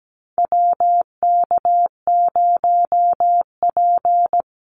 Morse code, heard as a keyed tone, about 17 wpm, 700 Hz. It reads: WK0P
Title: Call Signs - 17wpm